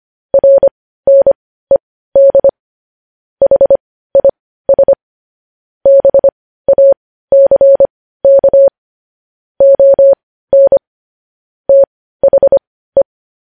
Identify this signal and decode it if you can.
Morse code, heard as a keyed tone, about 25 wpm, spaced as for 15 wpm, 550 Hz. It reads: RNED HIS BACK ON THE
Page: news-headlines-in-morse-code-at-15-wpm